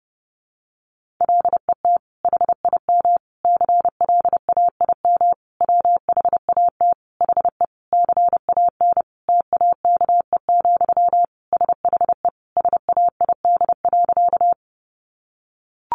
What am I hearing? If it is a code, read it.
Morse code, heard as a keyed tone, about 30 wpm, 700 Hz. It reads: LET HIM CLAIM WHAT HE CAN TAKE, SHE SAID.
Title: Sentences from Top 1500 Words 30wpm